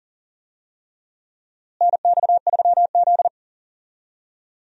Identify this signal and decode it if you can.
Morse code, heard as a keyed tone, about 40 wpm, 700 Hz. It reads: NX3Z
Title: USA Call Signs - 4 Characters - Repeated 3x - 40wpm